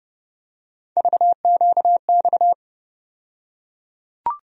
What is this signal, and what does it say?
Morse code, heard as a keyed tone, about 30 wpm, 700 Hz. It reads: VQX
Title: USA Call Sign Postfixes - 3 Letters 30wpm